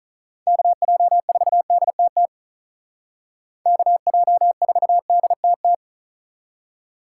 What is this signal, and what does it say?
Morse code, heard as a keyed tone, about 46 wpm, 700 Hz. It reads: KJ4DTT KJ4DTT